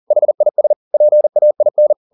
Morse code, heard as a keyed tone, about 40 wpm, 600 Hz. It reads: HIS PAIN
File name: the old man and the sea 2 40wpm